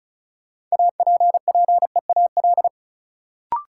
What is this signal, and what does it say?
Morse code, heard as a keyed tone, about 36 wpm, 700 Hz. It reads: APPEAL
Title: Sentences from Top 1300 Words - New Word Drill - 35wpm